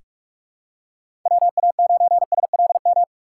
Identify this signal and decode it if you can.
Morse code, heard as a keyed tone, about 47 wpm, 700 Hz. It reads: WA9SLM
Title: USA Call Signs - 3x Character Spacing - 45wpm